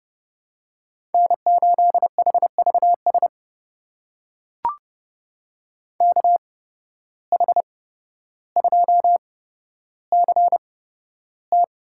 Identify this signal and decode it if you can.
Morse code, heard as a keyed tone, about 31 wpm, spaced as for 13 wpm, 700 Hz. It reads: N8HVS KH2CT
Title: USA Call Signs - 5 Characters - 8x Character Spacing - 30wpm